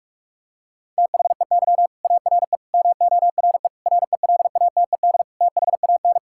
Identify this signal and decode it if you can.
Morse code, heard as a keyed tone, about 46 wpm, 700 Hz. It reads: THEY ARE MORE RELATED THAN
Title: Sentences from Top 1000 Words - Review Part 5 - 45wpm